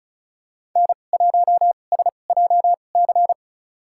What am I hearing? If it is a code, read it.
Morse code, heard as a keyed tone, about 36 wpm, spaced as for 25 wpm, 700 Hz. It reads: N1SJC
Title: USA Call Signs - 5 Characters - 2x Character Spacing - 35wpm